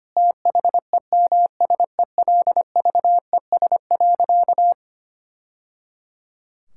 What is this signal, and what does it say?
Morse code, heard as a keyed tone, about 25 wpm, 700 Hz. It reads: THEMSELVES.
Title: Sentences from Top 1000 Words - Review Part 5 - 25wpm